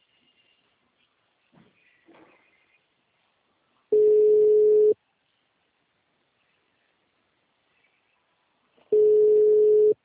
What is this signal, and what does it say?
Ringback tone (Europe, 425 Hz).
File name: out- -2002- - - .67690